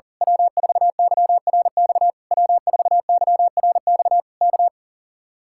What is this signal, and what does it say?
Morse code, heard as a keyed tone, about 38 wpm, 700 Hz. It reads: W4YRX W4YRX K
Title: Speed-Racing Common QSO Phrases Encore - 40wpm